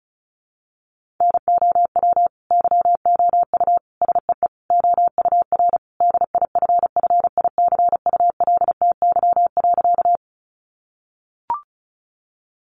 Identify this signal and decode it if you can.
Morse code, heard as a keyed tone, about 34 wpm, 700 Hz. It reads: NOW YOU SEE OUR DIFFICULTY.
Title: Sentences from Top 1500 Words 35wpm